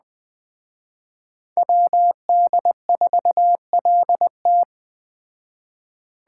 Morse code, heard as a keyed tone, about 20 wpm, 700 Hz. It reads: WD4LT